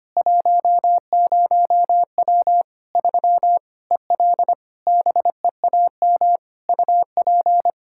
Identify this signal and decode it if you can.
Morse code, heard as a keyed tone, about 25 wpm, 700 Hz. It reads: 10W 3 EL BEAM UP
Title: K1USN SST QSO Practice 25wpm